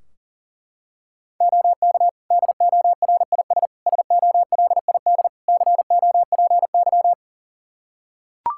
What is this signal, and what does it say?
Morse code, heard as a keyed tone, about 40 wpm, 700 Hz. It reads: OK DORIS SOLID COPY
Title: Practice QSO - Exchange 3 - More Info - 40wpm